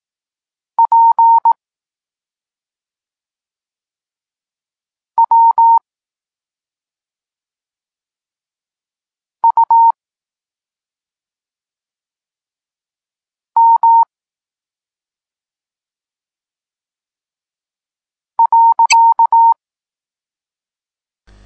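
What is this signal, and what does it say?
Morse code, heard as a keyed tone, about 18 wpm, 920 Hz. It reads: P W U M .
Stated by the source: Morse code lesson for learning the letters K and X, from the SP9MOA Niepołomicki Klub Krótkofalowców course.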